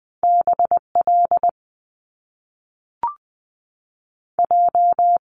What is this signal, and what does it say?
Morse code, heard as a keyed tone, about 20 wpm, 700 Hz. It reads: B L J
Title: Top Two Letter Combinations in Words 20wpm